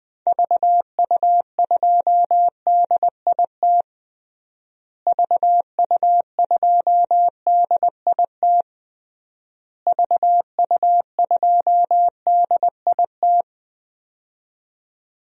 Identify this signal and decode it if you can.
Morse code, heard as a keyed tone, about 20 wpm, 700 Hz. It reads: VU2DIT VU2DIT VU2DIT
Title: DXCC-100 Call Signs - 6 Characters - Repeated 3x - 20wpm